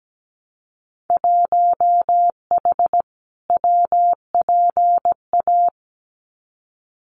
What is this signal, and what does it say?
Morse code, heard as a keyed tone, about 17 wpm, 700 Hz. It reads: 1H WPA
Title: Winter Field Day Exchanges 17wpm